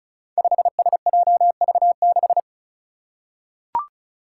Morse code, heard as a keyed tone, about 35 wpm, 700 Hz. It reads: 5SJV6
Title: Five Challenging Letter-Number Combinations 35wpm